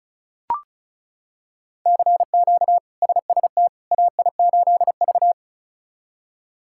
Morse code, heard as a keyed tone, about 34 wpm, 700 Hz. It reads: CQ SST AI8V
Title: K1USN SST QSO Practice - 35wpm